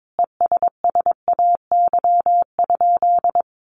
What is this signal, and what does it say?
Morse code, heard as a keyed tone, about 22 wpm, 700 Hz. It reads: ESSAY?
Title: Sentences from Top 1500 Words 22wpm